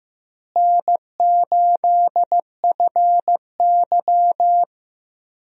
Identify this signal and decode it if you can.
Morse code, heard as a keyed tone, about 15 wpm, 700 Hz. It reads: N8FY
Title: USA Call Signs - 4 Characters - Repeated 3x - 15wpm